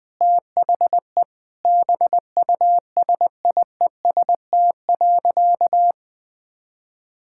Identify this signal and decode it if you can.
Morse code, heard as a keyed tone, about 20 wpm, 700 Hz. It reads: THE BUSIEST.
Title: Sentences with 7-Word Phrases - 20wpm